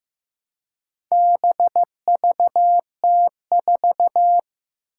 Morse code, heard as a keyed tone, about 15 wpm, 700 Hz. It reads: BVT4
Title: Four Challenging Letter-Number Combinations 15wpm